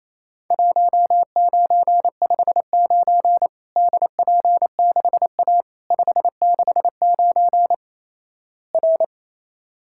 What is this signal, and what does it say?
Morse code, heard as a keyed tone, about 28 wpm, 700 Hz. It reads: 1959 DP6A 569 R
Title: Worked All Europe DX Contest - QTC Exchange - 28wpm